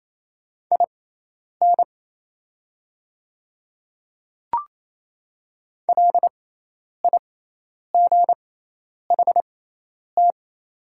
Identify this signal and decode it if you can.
Morse code, heard as a keyed tone, about 28 wpm, spaced as for 10 wpm, 700 Hz. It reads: IN LIGHT
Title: 6x Character Spacing - Top 200 Words - 28wpm